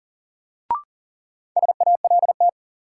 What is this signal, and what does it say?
Morse code, heard as a keyed tone, about 40 wpm, 700 Hz. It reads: SALT
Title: Sets of 5 Words - New Word Drill - 40wpm